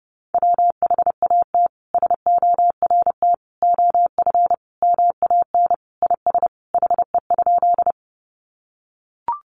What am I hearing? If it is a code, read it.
Morse code, heard as a keyed tone, about 28 wpm, 700 Hz. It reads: WHAT SORT OF MAN IS HE?
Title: Sentences from Top 1300 Words 30wpm